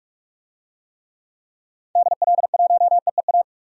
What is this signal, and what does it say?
Morse code, heard as a keyed tone, about 46 wpm, 700 Hz. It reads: DL1EEA
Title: DXCC-100 Call Signs - 6 Characters - 45wpm